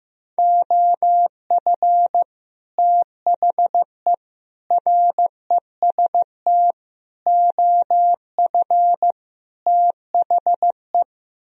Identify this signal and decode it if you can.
Morse code, heard as a keyed tone, about 15 wpm, 700 Hz. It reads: OF THE REST OF THE
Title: Speed-Racing - Sets of 5 Words 15wpm